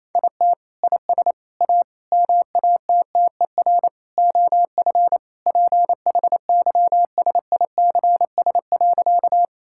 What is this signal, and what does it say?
Morse code, heard as a keyed tone, about 28 wpm, 700 Hz. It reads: IT IS A MATTER OF PHYSICS.
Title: Sentences from Top 1000 Words - Review Part 5 - 28wpm